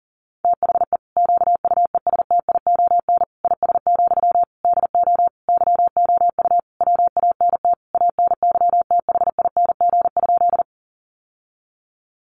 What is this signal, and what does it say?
Morse code, heard as a keyed tone, about 37 wpm, 700 Hz. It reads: THE QUESTION IS, DO YOU WANT ANYTHING?
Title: Sentences from Top 1000 Words 40wpm